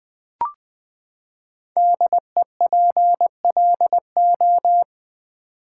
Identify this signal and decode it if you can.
Morse code, heard as a keyed tone, about 20 wpm, 700 Hz. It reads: DEPLO
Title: Build A Word - 5 Letters At A Time - 10-Letter Words - 20wpm